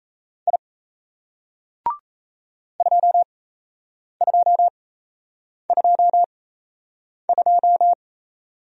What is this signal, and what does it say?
Morse code, heard as a keyed tone, about 37 wpm, spaced as for 8 wpm, 700 Hz. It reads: I 2222